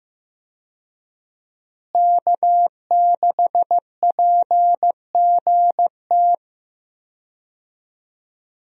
Morse code, heard as a keyed tone, about 15 wpm, 700 Hz. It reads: K6PGT
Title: USA Call Signs - 5 Characters - Repeated 3x - 15wpm